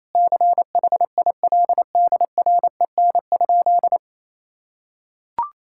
Morse code, heard as a keyed tone, about 28 wpm, 700 Hz. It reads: CHILDREN?
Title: Sentences from Top 1000 Words - Review Part 6 - 28wpm